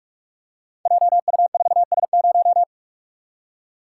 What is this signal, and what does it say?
Morse code, heard as a keyed tone, about 44 wpm, 690 Hz. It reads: JU4S0
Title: Code Groups - 5 Letter:Numbers - 45wpm